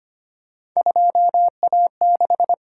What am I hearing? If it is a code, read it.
Morse code, heard as a keyed tone, about 29 wpm, 700 Hz. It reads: 2A6
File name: Speed-Racing - Three Letter-Number Combinations - 25wpm